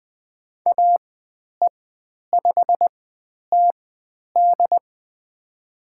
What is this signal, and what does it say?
Morse code, heard as a keyed tone, about 20 wpm, spaced as for 10 wpm, 700 Hz. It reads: AE5TD
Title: 102 Lesson 17g - Letter u - Call Signs 20wpm